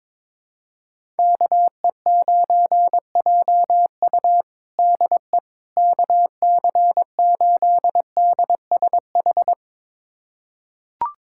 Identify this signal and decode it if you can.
Morse code, heard as a keyed tone, about 21 wpm, 700 Hz. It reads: KE9JU DE KC8DSH
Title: USA Callsign DE Callsign 22wpm